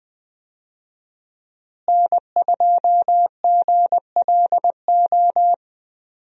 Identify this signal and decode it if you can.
Morse code, heard as a keyed tone, about 20 wpm, 700 Hz. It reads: N2GLO
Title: USA Call Signs - 5 Characters - Repeated 2x - 20wpm